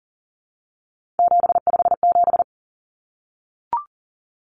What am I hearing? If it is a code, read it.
Morse code, heard as a keyed tone, about 40 wpm, 700 Hz. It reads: 757
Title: Common Cut Numbers - 3-Digit 40wpm